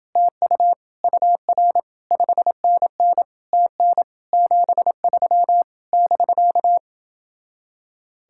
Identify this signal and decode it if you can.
Morse code, heard as a keyed tone, about 29 wpm, 700 Hz. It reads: TU UR 5NN TN 73 <BK>